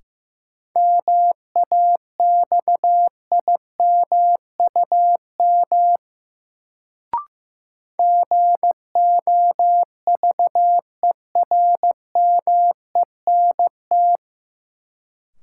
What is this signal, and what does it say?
Morse code, heard as a keyed tone, about 15 wpm, 700 Hz. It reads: MAXIMUM GOVERMENT
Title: Sentences from Top 1300 Words - New Word Drill Alt - 15wpm